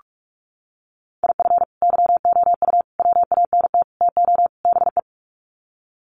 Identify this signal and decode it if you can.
Morse code, heard as a keyed tone, about 44 wpm, 700 Hz. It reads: IF YOU WANT TO BE
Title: Sets of 5 Words Encore 45wpm